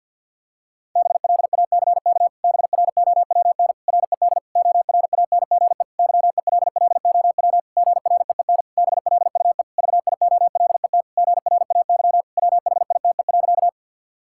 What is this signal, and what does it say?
Morse code, heard as a keyed tone, about 52 wpm, 700 Hz. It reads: BLACK BROWN RED ORANGE YELLOW GREEN BLUE VIOLET GRAY WHITE.
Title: Pangram Challenge v2 - 50wpm